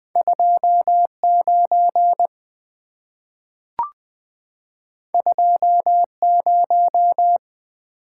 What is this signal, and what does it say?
Morse code, heard as a keyed tone, about 20 wpm, 690 Hz. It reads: 29 20
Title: Two Letter-Number Combinations - 20wpm